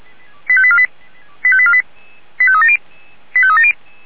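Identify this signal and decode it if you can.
Five-tone selective call (ZVEI), heard as quick runs of five tones, tones of about 70 ms. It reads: 85848 85848 85370 85370
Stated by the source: ZVEI five-tone selective call received by SDR.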